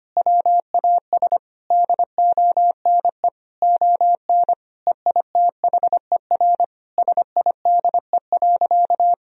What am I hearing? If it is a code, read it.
Morse code, heard as a keyed tone, about 25 wpm, 700 Hz. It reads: WAS DONE ON EITHER SIDE.
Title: Sentences from Top 1000 Words - Review Part 4 - 25wpm